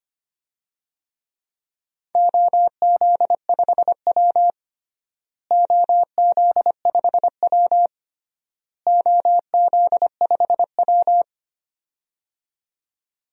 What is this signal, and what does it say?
Morse code, heard as a keyed tone, about 25 wpm, 700 Hz. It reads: OZ5W OZ5W OZ5W
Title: DXCC-100 Call Signs - 4 Characters - Repeated 3x - 25wpm